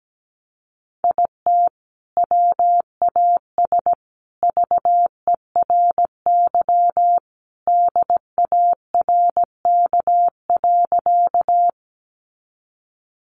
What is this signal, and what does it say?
Morse code, heard as a keyed tone, about 17 wpm, 700 Hz. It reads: IT WAS VERY DARK.
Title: Sentences from Top 500 Words 17wpm